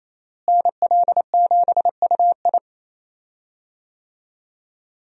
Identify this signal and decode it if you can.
Morse code, heard as a keyed tone, about 28 wpm, 700 Hz. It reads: NL7UI
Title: USA Call Signs - 5 Characters - Repeated 2x - 28wpm